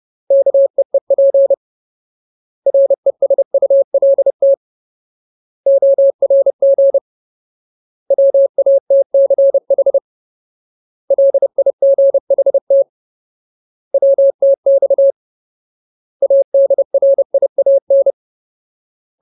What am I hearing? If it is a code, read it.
Morse code, heard as a keyed tone, about 30 wpm, 550 Hz. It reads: KEEP RESULT ORG WATCH LIGHT WTX ADRIAN
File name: Ultimate Word List / 30WPM Ultimate